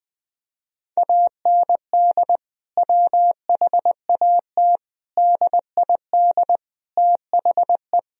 Morse code, heard as a keyed tone, about 20 wpm, 700 Hz. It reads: AND WHAT DID THE
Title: Sentences from Top 1000 Words - Review Part 2 - 20wpm